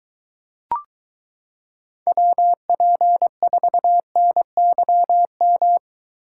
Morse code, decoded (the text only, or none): WP4NYM